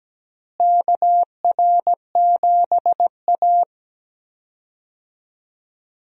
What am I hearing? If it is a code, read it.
Morse code, heard as a keyed tone, about 17 wpm, 700 Hz. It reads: KR7A
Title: USA Call Signs - 4 Characters - Repeated 2x - 17wpm